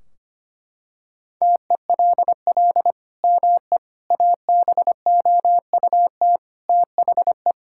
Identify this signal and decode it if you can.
Morse code, heard as a keyed tone, about 25 wpm, 700 Hz. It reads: TELL ME ABOUT THE
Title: Sentences from Top 1600 Words 25wpm